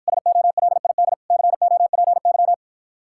Morse code, heard as a keyed tone, about 53 wpm, 700 Hz. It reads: SOLID COPY